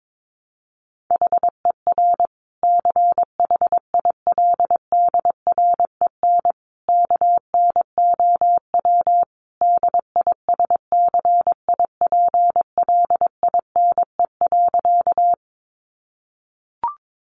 Morse code, heard as a keyed tone, about 22 wpm, 700 Hz. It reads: HER CHILDREN KNOW DISCIPLINE.
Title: Sentences from Top 900 Words 22wpm